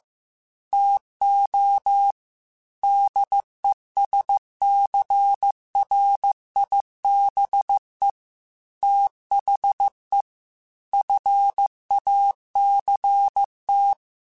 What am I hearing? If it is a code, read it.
Morse code, heard as a keyed tone, about 15 wpm, 800 Hz. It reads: TO DESCRIBE THE FACT
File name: Technical Subjects / Technical Subjects-mp3s / Reactance @13